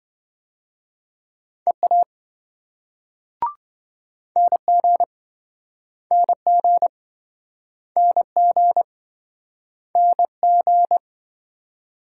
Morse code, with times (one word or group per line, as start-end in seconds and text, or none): 1.7-2.0 EA
4.4-5.0 NG
6.1-6.9 NG
8.0-8.8 NG
10.0-11.0 NG